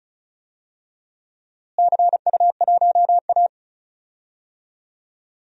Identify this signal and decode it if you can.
Morse code, heard as a keyed tone, about 36 wpm, 690 Hz. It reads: CU1A
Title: DXCC-100 Call Signs - 4 Characters - Repeated 3x - 35wpm